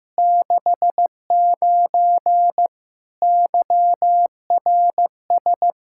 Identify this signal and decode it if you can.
Morse code, heard as a keyed tone, about 15 wpm, 700 Hz. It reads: 69 YRS
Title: K1USN SST QSO Practice 15wpm